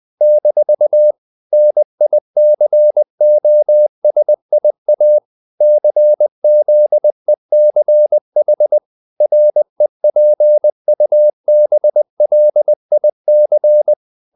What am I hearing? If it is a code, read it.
Morse code, heard as a keyed tone, about 20 wpm, 600 Hz. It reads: - NICOSIA CZECH REPUBLIC
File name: CW / Practice / NationalCapitals20